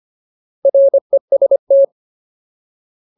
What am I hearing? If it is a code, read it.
Morse code, heard as a keyed tone, about 25 wpm, 550 Hz. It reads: REST
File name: Ultimate Word List / 25WPM Ultimate